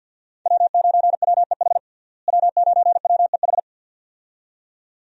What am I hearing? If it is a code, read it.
Morse code, heard as a keyed tone, about 48 wpm, 700 Hz. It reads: W9WEH W9WEH
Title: USA Call Signs - 5 Characters - Repeated 2x - 50wpm